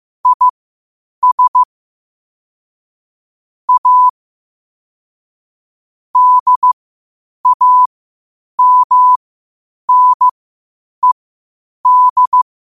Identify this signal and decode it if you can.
Morse code, heard as a keyed tone, about 15 wpm, spaced as for 8 wpm, 1000 Hz. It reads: IS A DAMNED
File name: qotdfiles / 8wpm